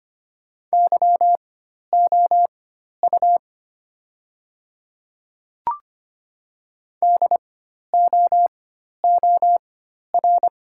Morse code, heard as a keyed tone, about 25 wpm, spaced as for 12 wpm, 700 Hz. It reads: YOU DOOR
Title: 4x Character Spacing - Top 300 Words - 25wpm